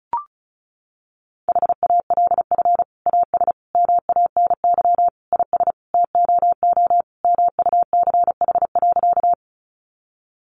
Morse code, heard as a keyed tone, about 34 wpm, 700 Hz. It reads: HALF AS MANY IS TOO MUCH.
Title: Sentences from Top 400 Words 35wpm